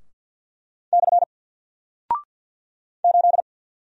Morse code, heard as a keyed tone, about 50 wpm, 700 Hz. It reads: / 8
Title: Single Character (Letter:Number:Punctuation:Prosign) 50wpm